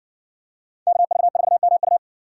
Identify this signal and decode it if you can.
Morse code, heard as a keyed tone, about 60 wpm, 700 Hz. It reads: KV4MU